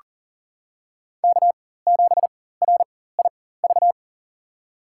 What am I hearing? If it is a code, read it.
Morse code, heard as a keyed tone, about 40 wpm, spaced as for 19 wpm, 700 Hz. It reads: K7RIV
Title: USA Call Signs - 5 Characters - 4x Character Spacing - 40wpm